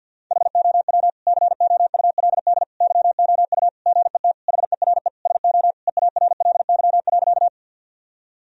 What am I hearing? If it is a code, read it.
Morse code, heard as a keyed tone, about 50 wpm, 700 Hz. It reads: HOW COULD YOU GET HERE SO EARLY.
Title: Sentences from Top 1000 Words - Review Part 2 - 50wpm